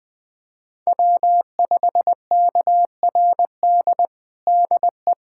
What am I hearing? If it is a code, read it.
Morse code, heard as a keyed tone, about 21 wpm, 700 Hz. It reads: W5KRD DE
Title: USA Callsign DE Callsign - 5 Characters - 20wpm